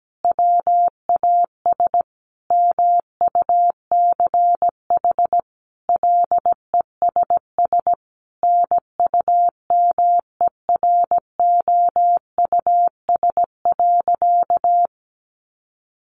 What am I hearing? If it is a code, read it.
Morse code, heard as a keyed tone, about 17 wpm, 700 Hz. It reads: WAS MUCH LESS NUMEROUS.
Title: Sentences from Top 1400 Words 17wpm